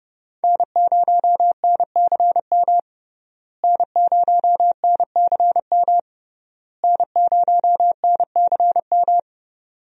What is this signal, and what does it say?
Morse code, heard as a keyed tone, about 32 wpm, 700 Hz. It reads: N0NCM N0NCM N0NCM
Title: USA Call Signs - 5 Characters - Repeated 3x - 30wpm